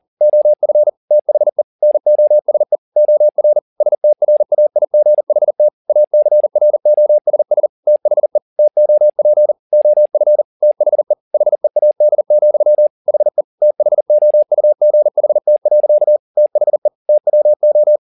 Morse code, heard as a keyed tone, about 40 wpm, 600 Hz. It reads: OF THE NOSE OR STRAIGHT ACROSS THE TOP OF THE HEAD, HE THOUGHT. THE TWO
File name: the old man and the sea 2 40wpm